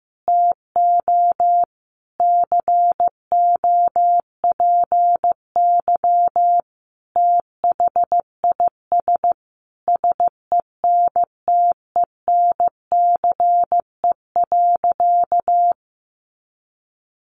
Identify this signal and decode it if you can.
Morse code, heard as a keyed tone, about 15 wpm, 700 Hz. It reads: TO COPY THIS SENTENCE.
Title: Additional Easy Phrases 15wpm